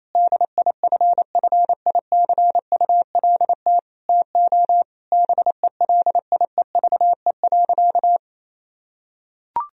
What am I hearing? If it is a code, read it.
Morse code, heard as a keyed tone, about 28 wpm, 700 Hz. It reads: DIFFICULT TO BELIEVE.
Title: Sentences from Top 1000 Words - Review Part 4 - 28wpm